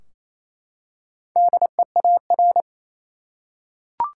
Morse code, heard as a keyed tone, about 28 wpm, 700 Hz. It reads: DEAR
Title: Four Letter Words - 28wpm